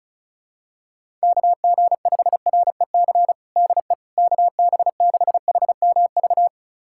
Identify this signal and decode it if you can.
Morse code, heard as a keyed tone, about 34 wpm, 700 Hz. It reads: KG5REC DE KB6HMV
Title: USA Callsign DE Callsign - 6 Characters - 35wpm